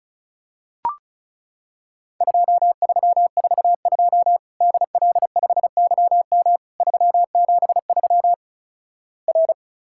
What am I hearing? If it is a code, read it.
Morse code, heard as a keyed tone, about 36 wpm, 700 Hz. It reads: 2342 DL5YM 373 R
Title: Worked All Europe DX Contest - QTC Exchange - 35wpm